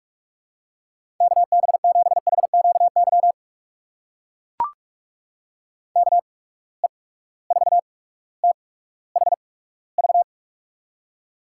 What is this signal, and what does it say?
Morse code, heard as a keyed tone, about 44 wpm, spaced as for 19 wpm, 700 Hz. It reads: KB7HQY KE4THV
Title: USA Call Signs - 6 Characters - 8x Character Spacing - 45wpm